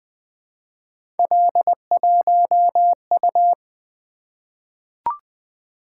Morse code, heard as a keyed tone, about 20 wpm, 700 Hz. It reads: L1U
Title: Three Letter-Number Combinations - 20wpm